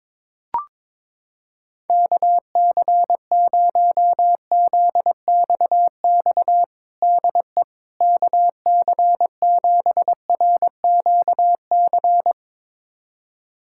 Morse code, heard as a keyed tone, about 22 wpm, 700 Hz. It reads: KC0ZXX DE KC7RQC
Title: USA Callsign DE Callsign 22wpm